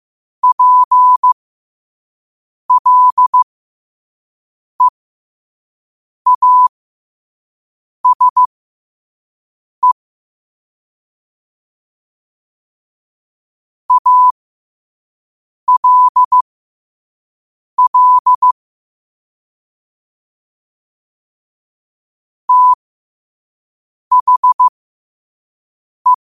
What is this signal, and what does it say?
Morse code, heard as a keyed tone, about 15 wpm, spaced as for 5 wpm, 1000 Hz. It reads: PLEASE ALL THE
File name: qotdfiles / 5wpm